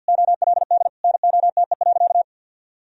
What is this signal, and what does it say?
Morse code, heard as a keyed tone, about 46 wpm, 700 Hz. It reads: OLD NOTE.
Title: Sentences from Top 1000 Words - Review Part 2 - 50wpm